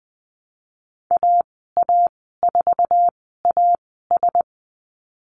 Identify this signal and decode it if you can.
Morse code, heard as a keyed tone, about 20 wpm, 700 Hz. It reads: A A 4 A S
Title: 048 Lesson 9g - Number 4 - Call Signs 20wpm